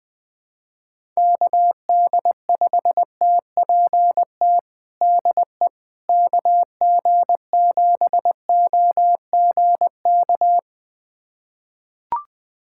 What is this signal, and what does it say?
Morse code, heard as a keyed tone, about 21 wpm, 700 Hz. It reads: KD5TPT DE KG7OGK
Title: USA Callsign DE Callsign 20wpm